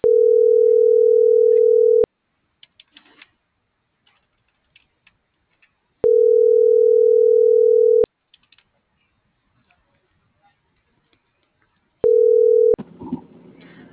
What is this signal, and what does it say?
Ringback tone (North America), 440 + 480 Hz.